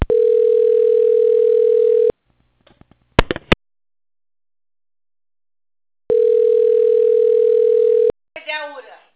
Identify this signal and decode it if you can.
Ringback tone (North America), 440 + 480 Hz, with speech.
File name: out- - - - - .51130